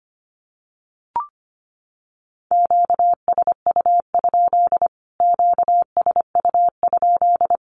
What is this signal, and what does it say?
Morse code, heard as a keyed tone, about 25 wpm, 700 Hz. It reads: QSU? QSU?
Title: Q-Codes Repeated 3x 25wpm